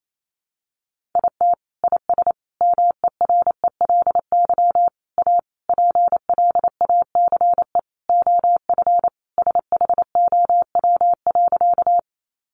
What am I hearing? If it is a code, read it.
Morse code, heard as a keyed tone, about 28 wpm, 700 Hz. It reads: IT IS MERELY A PLACE OF SHOW.
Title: Sentences from Top 1400 Words 28wpm